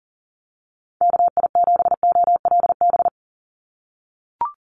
Morse code, heard as a keyed tone, about 40 wpm, 700 Hz. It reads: KI7OLB
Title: USA Call Signs 40wpm